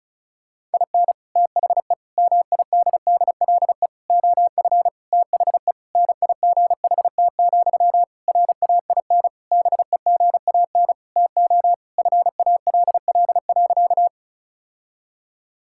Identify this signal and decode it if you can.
Morse code, heard as a keyed tone, about 34 wpm, 700 Hz. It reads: IN THE MIDDLE OF THE NIGHT, RAIN BEGAN TO FALL.
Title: Sentences from Top 1000 Words - Review Part 6 - 35wpm